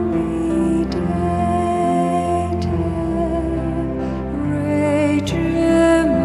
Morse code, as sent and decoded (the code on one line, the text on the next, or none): none